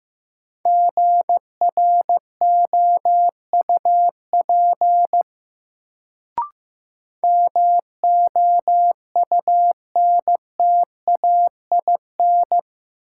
Morse code, heard as a keyed tone, about 15 wpm, 700 Hz. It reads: GROUP MOUNTAIN
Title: Sentences from Top 1000 Words - Review Part 1 - New Word Drill - 15wpm